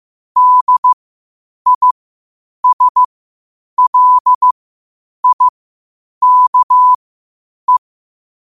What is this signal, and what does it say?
Morse code, heard as a keyed tone, about 15 wpm, spaced as for 8 wpm, 1000 Hz. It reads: DISLIKE
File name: qotdfiles / 8wpm